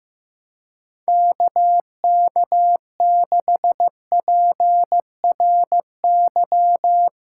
Morse code, heard as a keyed tone, about 15 wpm, 700 Hz. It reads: KK6PRY
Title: USA Callsign DE Callsign - 6 Characters - 15wpm